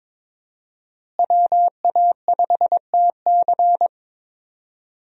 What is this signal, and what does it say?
Morse code, heard as a keyed tone, about 21 wpm, 700 Hz. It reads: WA5TC
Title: USA Call Signs - 5 Characters - Repeated 3x - 22wpm